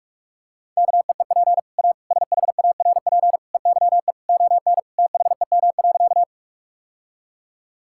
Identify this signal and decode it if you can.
Morse code, heard as a keyed tone, about 46 wpm, 700 Hz. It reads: KEEP A SHARP EYE ON THEM.
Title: Sentences from Top 1600 Words 45wpm